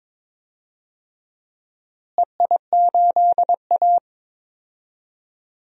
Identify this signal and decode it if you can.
Morse code, heard as a keyed tone, about 22 wpm, 700 Hz. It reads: EI8A
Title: DXCC-100 Call Signs - 4 Characters - Repeated 2x - 22wpm